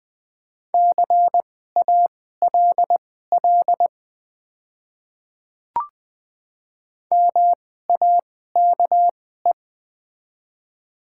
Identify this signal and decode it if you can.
Morse code, heard as a keyed tone, about 20 wpm, spaced as for 14 wpm, 700 Hz. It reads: CALL MAKE
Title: 2x Character Spacing - Top 200 Words - 20wpm